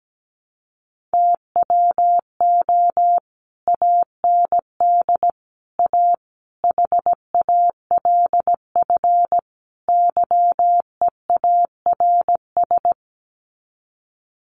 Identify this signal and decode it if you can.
Morse code, heard as a keyed tone, about 17 wpm, 700 Hz. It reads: TWO AND A HALF YEARS
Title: Sets of 5 Words 17wpm